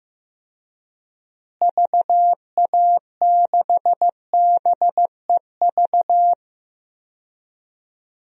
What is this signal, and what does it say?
Morse code, heard as a keyed tone, about 15 wpm, 700 Hz. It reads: VA6BEV
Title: DXCC-100 Call Signs - 6 Characters - Repeated 3x - 15wpm